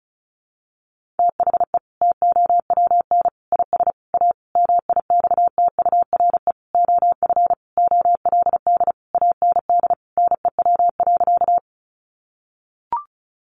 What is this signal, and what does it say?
Morse code, heard as a keyed tone, about 34 wpm, 700 Hz. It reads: THE TOWN IS A MIXTURE OF OLD AND NEW.
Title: Sentences from Top 1500 Words 35wpm